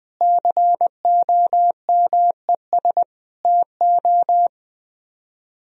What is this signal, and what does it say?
Morse code, heard as a keyed tone, about 20 wpm, 700 Hz. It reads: COMES TO